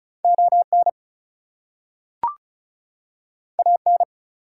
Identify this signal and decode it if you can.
Morse code, heard as a keyed tone, about 36 wpm, 700 Hz. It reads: ON AN
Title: Two Letter Words - 35wpm